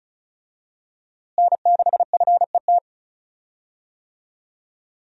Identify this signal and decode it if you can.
Morse code, heard as a keyed tone, about 34 wpm, 700 Hz. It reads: N6FET
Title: USA Call Signs - 5 Characters - Repeated 2x - 35wpm